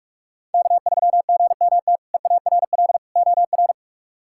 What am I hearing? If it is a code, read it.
Morse code, heard as a keyed tone, about 43 wpm, 700 Hz. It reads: K3GMT EARL OR
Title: K1USN SST QSO Practice - 45wpm